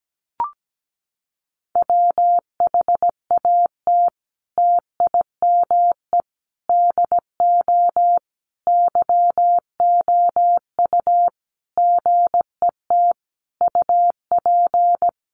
Morse code, heard as a keyed tone, about 17 wpm, 700 Hz. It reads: WHAT TIME DO YOU GET UP
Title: Sentences from Top 1500 Words 17wpm